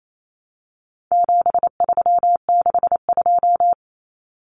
Morse code, heard as a keyed tone, about 28 wpm, 700 Hz. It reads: 7362
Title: Numbers - 4-Digit 28wpm